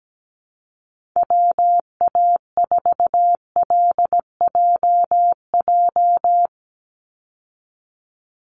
Morse code, heard as a keyed tone, about 17 wpm, 700 Hz. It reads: WA4LJJ
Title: Call Signs Repeated 3x - 17wpm